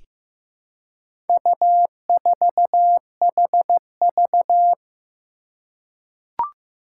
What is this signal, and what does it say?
Morse code, heard as a keyed tone, about 15 wpm, 700 Hz. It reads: U4HV
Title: Four Challenging Letter-Number Combinations 15wpm